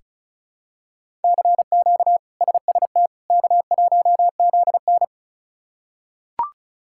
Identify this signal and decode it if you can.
Morse code, heard as a keyed tone, about 36 wpm, 700 Hz. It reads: CQ SST K1ZN
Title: K1USN SST QSO Practice - 35wpm